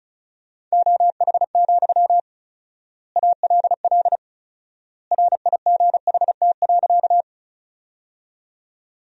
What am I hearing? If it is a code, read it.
Morse code, heard as a keyed tone, about 34 wpm, 700 Hz. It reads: OH, ALL RIGHT.
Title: 4x Word-Spacing - Sentences from Top 500 Words - 35wpm